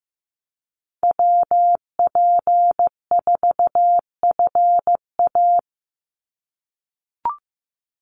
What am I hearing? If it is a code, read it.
Morse code, heard as a keyed tone, about 15 wpm, 700 Hz. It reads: WP4FA
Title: USA Call Signs 15wpm